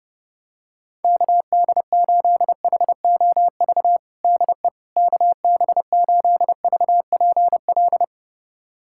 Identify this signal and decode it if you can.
Morse code, heard as a keyed tone, about 30 wpm, 700 Hz. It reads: KD8HOV DE KB8VPL
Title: USA Callsign DE Callsign - 6 Characters - 30wpm